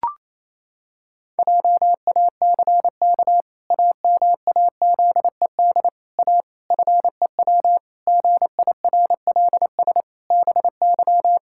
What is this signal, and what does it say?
Morse code, heard as a keyed tone, about 28 wpm, 700 Hz. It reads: JACK AMAZED A FEW GIRLS BY
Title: Pangram Challenge v2 - 28wpm